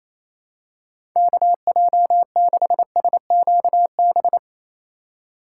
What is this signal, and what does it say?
Morse code, heard as a keyed tone, about 28 wpm, 700 Hz. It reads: KJ6SQB
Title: USA Call Signs - 6 Characters - 2x Character Spacing - 28wpm